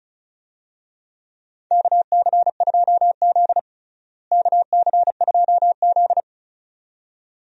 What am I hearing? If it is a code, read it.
Morse code, heard as a keyed tone, about 36 wpm, 700 Hz. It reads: KC2Z KC2Z
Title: USA Call Signs - 4 Characters - Repeated 2x - 35wpm